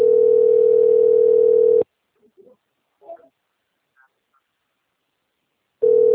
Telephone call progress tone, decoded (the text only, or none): ringback tone